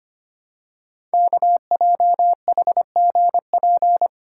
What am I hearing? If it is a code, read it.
Morse code, heard as a keyed tone, about 25 wpm, 700 Hz. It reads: KJHGP
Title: Code Groups - 5 Letters - 25wpm